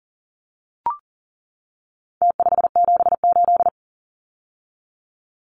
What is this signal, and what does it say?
Morse code, heard as a keyed tone, about 40 wpm, 700 Hz. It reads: T578
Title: Numbers - Contest Style - 4-Digit 40wpm